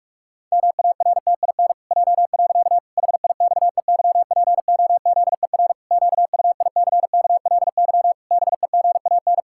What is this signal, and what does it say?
Morse code, heard as a keyed tone, about 48 wpm, 700 Hz. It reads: MARTIN J. HIXEYPOZER QUICKLY BEGAN
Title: Pangram Challenge v2 - 45wpm